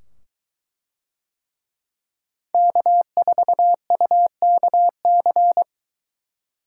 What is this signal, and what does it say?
Morse code, heard as a keyed tone, about 22 wpm, 700 Hz. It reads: K4UKC